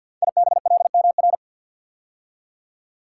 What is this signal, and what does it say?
Morse code, heard as a keyed tone, about 48 wpm, 700 Hz. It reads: IBLMR
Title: Code Groups - 5 Letters - 50wpm